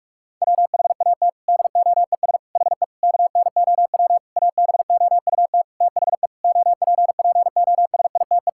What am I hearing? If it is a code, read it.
Morse code, heard as a keyed tone, about 41 wpm, 700 Hz. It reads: WHAT DOES HE KNOW ABOUT THE OPPOSITE
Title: Sentences from Top 1600 Words 45wpm